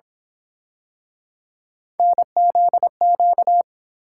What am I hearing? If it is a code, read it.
Morse code, heard as a keyed tone, about 30 wpm, 700 Hz. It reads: NZQ